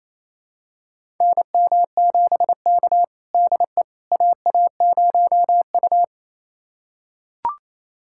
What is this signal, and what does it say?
Morse code, heard as a keyed tone, about 28 wpm, 700 Hz. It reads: NM7K DE AA0U
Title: USA Callsign DE Callsign - 4 Characters - 28wpm